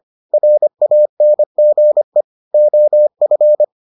Morse code, heard as a keyed tone, about 25 wpm, 600 Hz. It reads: RANGE OF